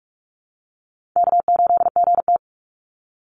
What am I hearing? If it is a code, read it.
Morse code, heard as a keyed tone, about 44 wpm, 700 Hz. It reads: K8GT
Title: Call Signs - 45wpm